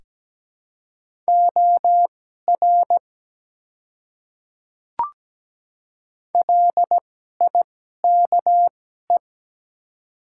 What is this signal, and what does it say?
Morse code, heard as a keyed tone, about 17 wpm, spaced as for 12 wpm, 700 Hz. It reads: OR LIKE
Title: 2x Character Spacing - Top 200 Words - 17wpm